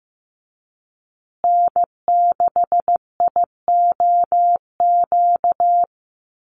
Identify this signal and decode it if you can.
Morse code, heard as a keyed tone, about 15 wpm, 700 Hz. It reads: N6IOQ
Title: USA Call Signs Repeated 3x 15wpm